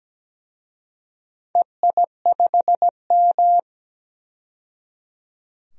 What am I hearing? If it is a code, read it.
Morse code, heard as a keyed tone, about 17 wpm, 700 Hz. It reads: EI5M
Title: DXCC-100 Call Signs - 4 Characters - 17wpm